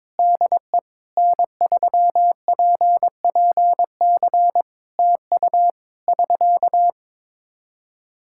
Morse code, heard as a keyed tone, about 22 wpm, 700 Hz. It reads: DE N3PPC TU <SK>
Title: Practice QSO - Exchange 4 - Ending - 22wpm